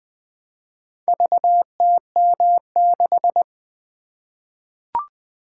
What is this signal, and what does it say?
Morse code, heard as a keyed tone, about 20 wpm, 700 Hz. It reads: VTM6
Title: Four Letter-Number Combinations - 20wpm